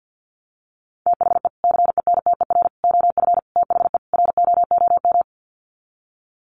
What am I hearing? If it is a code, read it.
Morse code, heard as a keyed tone, about 50 wpm, 700 Hz. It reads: THE CENTER OF THE ROOM
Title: Sets of 5 Words Encore 50wpm